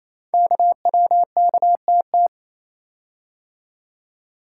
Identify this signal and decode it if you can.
Morse code, heard as a keyed tone, about 29 wpm, 700 Hz. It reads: KWKTT
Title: Code Groups - 5 Letters - 28wpm